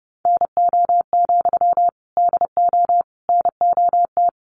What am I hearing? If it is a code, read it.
Morse code, heard as a keyed tone, about 30 wpm, 700 Hz. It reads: NO, DO NOT
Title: Sentences from Top 1200 Words 30wpm